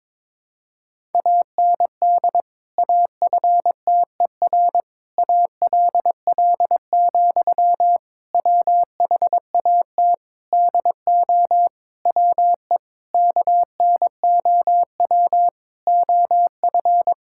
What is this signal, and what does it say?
Morse code, heard as a keyed tone, about 22 wpm, 700 Hz. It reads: AND AFTER ALL, WHAT DO WE KNOW OF
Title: Sentences from Top 1600 Words 22wpm